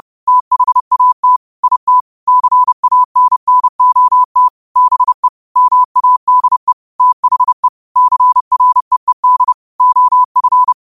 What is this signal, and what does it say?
Morse code, heard as a keyed tone, about 30 wpm, 1000 Hz. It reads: THAT IT CANNOT BE MADE THE CREED OF
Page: New quotes every day in morse code at 30 Words per minute.